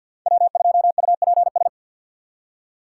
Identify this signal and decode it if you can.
Morse code, heard as a keyed tone, about 53 wpm, 700 Hz. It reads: W2UPS
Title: USA Call Signs - 2x Character Spacing - 50wpm